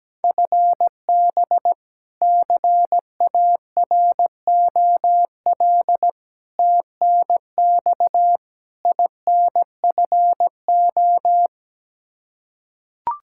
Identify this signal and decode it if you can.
Morse code, heard as a keyed tone, about 17 wpm, 700 Hz. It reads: FB CAROL TNX INFO
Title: Practice QSO - Exchange 3 - More Info - 17wpm